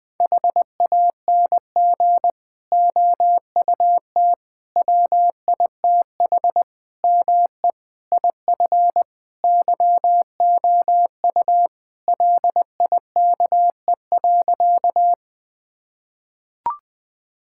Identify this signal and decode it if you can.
Morse code, heard as a keyed tone, about 20 wpm, 700 Hz. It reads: HANG OUT WITH ME IF YOU LIKE.
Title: Sentences from Top 1600 Words 20wpm